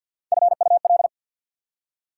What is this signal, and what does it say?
Morse code, heard as a keyed tone, about 65 wpm, spaced as for 57 wpm, 690 Hz. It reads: FUL
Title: Speed-Racing Prefixes and Suffixes 50wpm